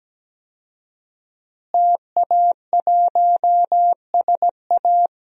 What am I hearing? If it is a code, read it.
Morse code, heard as a keyed tone, about 17 wpm, 700 Hz. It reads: TA1SA
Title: DXCC-100 Call Signs - 5 Characters - Repeated 2x - 17wpm